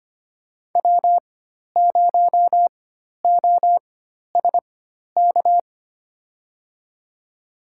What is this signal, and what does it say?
Morse code, heard as a keyed tone, about 25 wpm, spaced as for 12 wpm, 700 Hz. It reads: W0OSK
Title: USA Call Signs - 4x Character Spacing - 25wpm